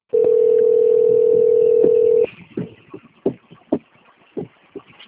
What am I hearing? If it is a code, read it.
Ringback tone (North America), 440 + 480 Hz.